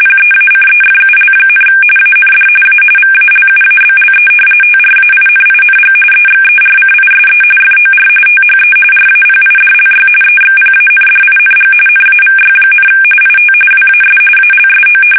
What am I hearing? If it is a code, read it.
RTTY, about 45.45 baud. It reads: PEACEFUL, AND A VALUABLE TRAINING EXERCISE FOR THE RAPIDLY EXPANDING NAVY, IT WAS ALSO
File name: fsk850 / usnavy3